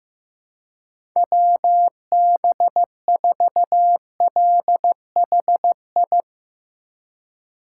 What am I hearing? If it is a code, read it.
Morse code, heard as a keyed tone, about 15 wpm, 700 Hz. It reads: WB4LHI
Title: USA Call Signs - 6 Characters - 6x Character Spacing - 15wpm